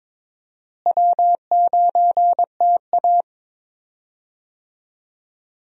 Morse code, heard as a keyed tone, about 22 wpm, 700 Hz. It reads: W9TA
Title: USA Call Signs - 4 Characters - Repeated 3x - 22wpm